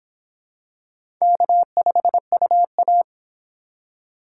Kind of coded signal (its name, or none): Morse code